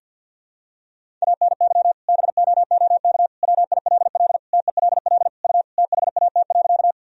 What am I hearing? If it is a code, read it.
Morse code, heard as a keyed tone, about 51 wpm, 700 Hz. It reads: ANY BOOK WILL TELL U THAT.
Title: Sentences from Top 300 Words and Abbreviations 50wpm